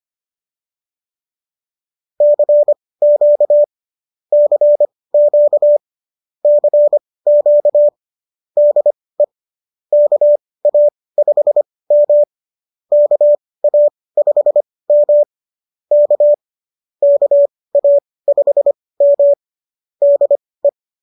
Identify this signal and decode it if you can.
Morse code, heard as a keyed tone, about 25 wpm, spaced as for 18 wpm, 590 Hz. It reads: CQ CQ CQ DE KA5M KA5M K KA5M DE